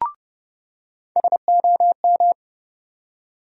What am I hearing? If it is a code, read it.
Morse code, heard as a keyed tone, about 30 wpm, 700 Hz. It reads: SOM
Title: USA Call Sign Postfixes - 3 Letters 30wpm